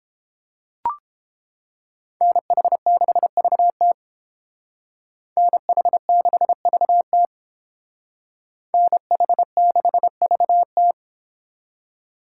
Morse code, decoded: NH6VT NH6VT NH6VT

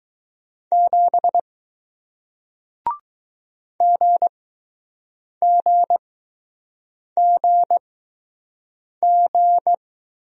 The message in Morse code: 7 GGGG

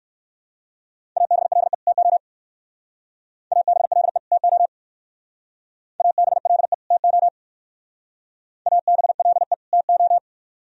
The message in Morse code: ABLE TO ABLE TO ABLE TO ABLE TO

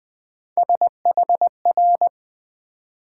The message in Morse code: SHR

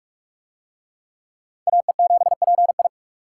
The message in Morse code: AE7PI

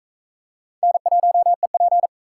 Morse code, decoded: N1EP

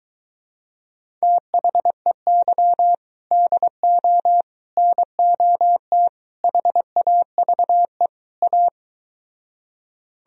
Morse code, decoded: THEY DO NOT HAVE A

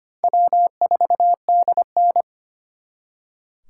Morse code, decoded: W4DN

